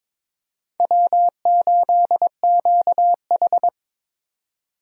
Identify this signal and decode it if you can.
Morse code, heard as a keyed tone, about 22 wpm, 700 Hz. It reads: W8QH
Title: USA Call Signs - 4 Characters - 22wpm